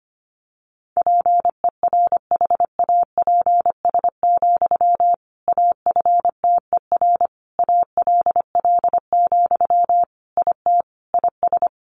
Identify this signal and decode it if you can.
Morse code, heard as a keyed tone, about 25 wpm, 700 Hz. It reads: PERHAPS, AFTER ALL, IT IS
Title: Sentences from Top 500 Words 25wpm